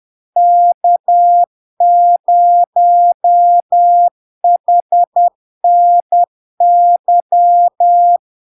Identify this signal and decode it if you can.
Morse code, heard as a keyed tone, about 10 wpm, 700 Hz. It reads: K0HNY
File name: LARK-FD-2022-Log-in-CW-10wpm-700-Hz